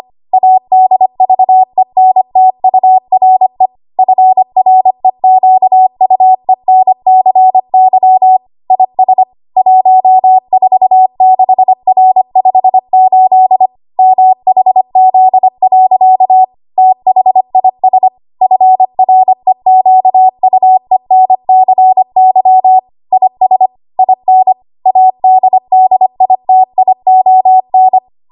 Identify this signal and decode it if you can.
Morse code, heard as a keyed tone, about 25 wpm, 750 Hz. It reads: ADVENTURE FREQUENCY IS 146R58 MHZ. THIS FREQUENCY IS IN ADDITION
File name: Morse / 25WPM